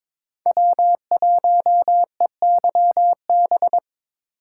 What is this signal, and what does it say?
Morse code, heard as a keyed tone, about 22 wpm, 700 Hz. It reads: W1EYB
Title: USA Call Signs - 5 Characters - 2x Character Spacing - 22wpm